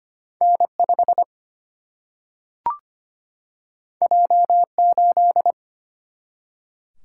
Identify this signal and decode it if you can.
Morse code, heard as a keyed tone, about 25 wpm, 700 Hz. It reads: N5 J8
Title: Single-Letter Single-Number 25wpm